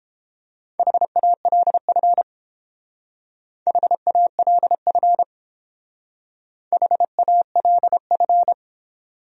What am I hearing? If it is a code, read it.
Morse code, heard as a keyed tone, about 29 wpm, 700 Hz. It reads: HALF HALF HALF